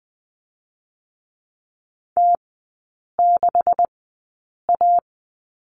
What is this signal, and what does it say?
Morse code, heard as a keyed tone, about 20 wpm, spaced as for 8 wpm, 700 Hz. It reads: T6A
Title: 191 Lesson 31e - Letter v - Three Characters 20wpm